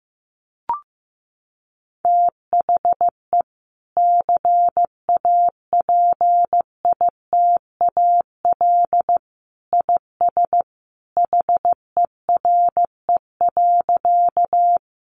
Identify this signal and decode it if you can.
Morse code, heard as a keyed tone, about 15 wpm, 700 Hz. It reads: THE CAPITAL IS HERE.
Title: Sentences from Top 1000 Words 15wpm